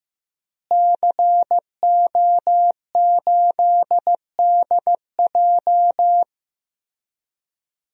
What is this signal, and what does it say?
Morse code, heard as a keyed tone, about 15 wpm, 700 Hz. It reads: CO8DJ
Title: DXCC-100 Call Signs - 5 Characters - Repeated 3x - 15wpm